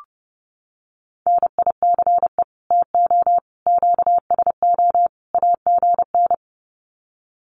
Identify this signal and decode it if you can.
Morse code, heard as a keyed tone, about 30 wpm, 700 Hz. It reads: NICE TO QSO AGN
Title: Common QSO Phrases 30wpm